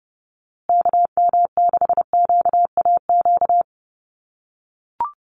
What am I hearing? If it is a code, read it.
Morse code, heard as a keyed tone, about 30 wpm, 700 Hz. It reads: KM6QAQ
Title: USA Call Signs Repeated 3x 30wpm